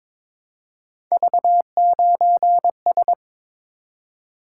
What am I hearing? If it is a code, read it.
Morse code, heard as a keyed tone, about 22 wpm, 700 Hz. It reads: V9S
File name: practice / 22wpm / Three Letter-Number Combinations - 22wpm